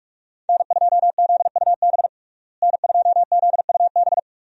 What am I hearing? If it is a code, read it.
Morse code, heard as a keyed tone, about 46 wpm, 700 Hz. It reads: N2ZUB N2ZUB
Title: USA Call Signs - 5 Characters - Repeated 3x - 45wpm